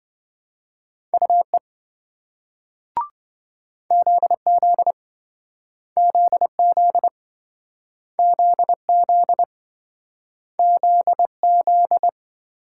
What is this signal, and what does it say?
Morse code, heard as a keyed tone, about 27 wpm, 700 Hz. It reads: UE ZZ ZZ ZZ ZZ